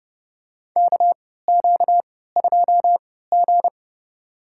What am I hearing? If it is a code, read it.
Morse code, heard as a keyed tone, about 30 wpm, spaced as for 17 wpm, 700 Hz. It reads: KQ2G
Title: USA Call Signs - 4 Characters - 3x Character Spacing - 30wpm